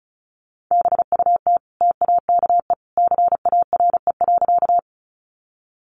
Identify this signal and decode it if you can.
Morse code, heard as a keyed tone, about 34 wpm, 700 Hz. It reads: BUT TAKE CARE.
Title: Sentences from Top 300 Words 35wpm